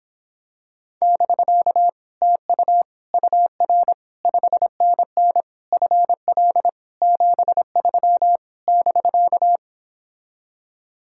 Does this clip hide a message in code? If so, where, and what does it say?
Yes: Morse code from 1.0 to 9.6 s, heard as a keyed tone, about 26 wpm, 700 Hz. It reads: <BK> TU UR 5NN FL 73 <BK>